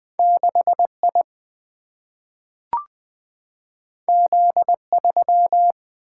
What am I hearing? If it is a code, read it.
Morse code, heard as a keyed tone, about 20 wpm, 700 Hz. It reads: 6I Z3
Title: Single-Letter Single-Number 20wpm